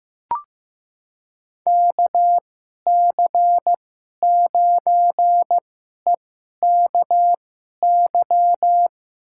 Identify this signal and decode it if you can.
Morse code, heard as a keyed tone, about 15 wpm, spaced as for 11 wpm, 700 Hz. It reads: KC9EKY
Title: USA Call Signs - 6 Characters - 2x Character Spacing - 15wpm